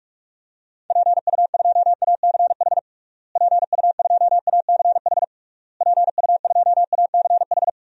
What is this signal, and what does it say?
Morse code, heard as a keyed tone, about 46 wpm, 700 Hz. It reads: PU2ACH PU2ACH PU2ACH
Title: DXCC-100 Call Signs - 6 Characters - Repeated 3x - 45wpm